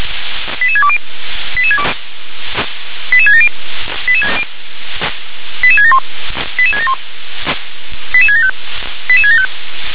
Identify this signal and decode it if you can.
Five-tone selective call (ZVEI), heard as quick runs of five tones, tones of about 70 ms. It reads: 88620 88670 88671 88671 88675 88675